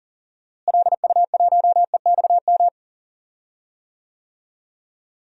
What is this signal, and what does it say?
Morse code, heard as a keyed tone, about 40 wpm, 700 Hz. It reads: LU1EXM
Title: DXCC-100 Call Signs - 6 Characters - Repeated 3x - 40wpm